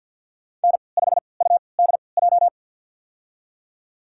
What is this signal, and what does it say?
Morse code, heard as a keyed tone, about 46 wpm, spaced as for 27 wpm, 700 Hz. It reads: N5UDJ
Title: USA Call Signs - 5 Characters - 3x Character Spacing - 50wpm